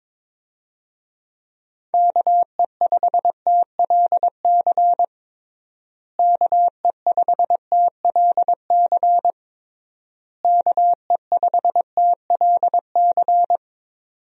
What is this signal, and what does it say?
Morse code, heard as a keyed tone, about 22 wpm, 700 Hz. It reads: KE5TLC KE5TLC KE5TLC
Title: USA Call Signs - 6 Characters - Repeated 3x - 22wpm